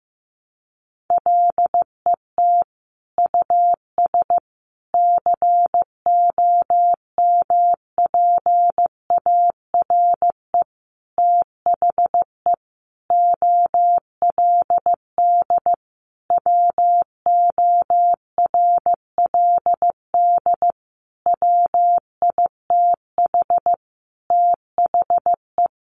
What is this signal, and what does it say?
Morse code, heard as a keyed tone, about 15 wpm, 700 Hz. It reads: LET US COMPARE THE OLD WORLD WITH THE
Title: Sentences from Top 1300 Words 15wpm